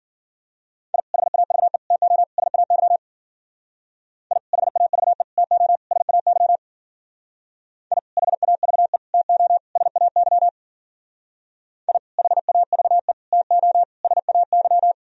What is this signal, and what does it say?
Morse code, heard as a keyed tone, about 48 wpm, 700 Hz. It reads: I HAVE TO SAY I HAVE TO SAY I HAVE TO SAY I HAVE TO SAY
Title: Speed-Racing - Sets of 4 Words 40wpm